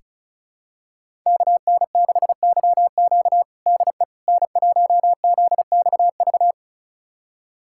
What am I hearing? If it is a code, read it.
Morse code, heard as a keyed tone, about 36 wpm, 700 Hz. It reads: KN6YQ DE N1ZXV
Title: USA Callsign DE Callsign - 5 Characters - 35wpm